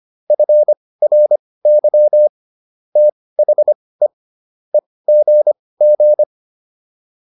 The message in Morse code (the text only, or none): FRY THE EGG